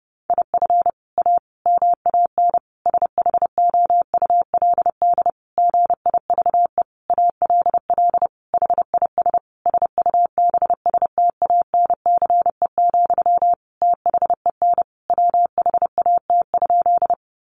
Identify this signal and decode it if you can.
Morse code, heard as a keyed tone, about 30 wpm, 700 Hz. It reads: IF A MAN SHOULD GIVE ALL HIS SUBSTANCE, THEN WHAT?
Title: Sentences from Top 1500 Words 30wpm